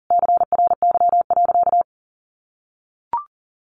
Morse code, heard as a keyed tone, about 40 wpm, 700 Hz. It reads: CRY.
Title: Sentences from Top 1500 Words 40wpm